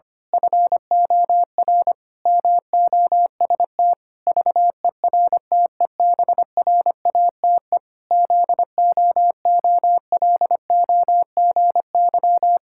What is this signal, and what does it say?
Morse code, heard as a keyed tone, about 25 wpm, 700 Hz. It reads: FOR MOST VERTEBRATE ZOOLOGY
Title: Pangram Challenge v2 - 25wpm